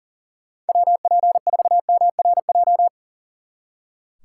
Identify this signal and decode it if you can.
Morse code, heard as a keyed tone, about 38 wpm, 700 Hz. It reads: WP4MRJ
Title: Speed Racing - USA 2x3 Call Signs 40wpm